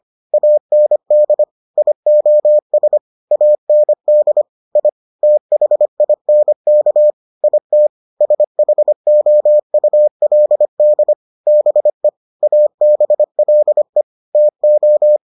Morse code, heard as a keyed tone, about 25 wpm, 600 Hz. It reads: AND IOS AND I THINK IT SHOULD BE ABLE TO